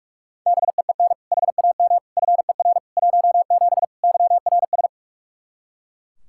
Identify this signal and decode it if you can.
Morse code, heard as a keyed tone, about 42 wpm, 700 Hz. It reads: BEEN HAM FER 17 YRS
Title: Practice QSO - Exchange 3 - More Info - 45wpm